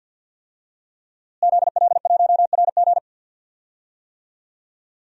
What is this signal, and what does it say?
Morse code, heard as a keyed tone, about 53 wpm, 700 Hz. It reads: ZL1RG
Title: DXCC-100 Call Signs - 5 Characters - Repeated 2x - 50wpm